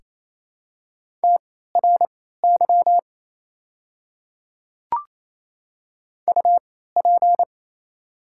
Morse code, heard as a keyed tone, about 29 wpm, spaced as for 16 wpm, 700 Hz. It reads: TRY UP
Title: 3x Character Spacing - Top 300 Words - 28wpm